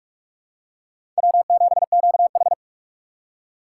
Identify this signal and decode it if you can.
Morse code, heard as a keyed tone, about 46 wpm, 700 Hz. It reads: W7QH
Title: USA Call Signs - 4 Characters - 3x Character Spacing - 45wpm